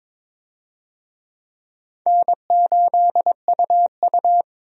Morse code, heard as a keyed tone, about 22 wpm, 700 Hz. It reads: N8UU
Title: USA Call Signs - 4 Characters - 22wpm